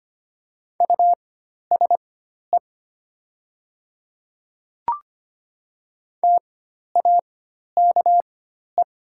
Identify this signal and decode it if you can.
Morse code, heard as a keyed tone, about 25 wpm, spaced as for 12 wpm, 690 Hz. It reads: USE TAKE
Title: 4x Character Spacing - Top 200 Words - 25wpm